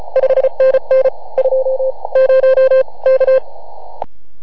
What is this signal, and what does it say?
Morse code, heard as a keyed tone, about 32 wpm, spaced as for 27 wpm, 520 Hz. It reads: 5NN 20 K